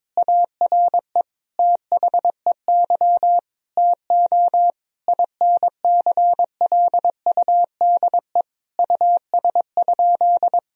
Morse code, heard as a keyed tone, about 22 wpm, 700 Hz. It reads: ARE THEY TO INCLUDE US?
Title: Sentences from Top 1000 Words - Review Part 3 - 22wpm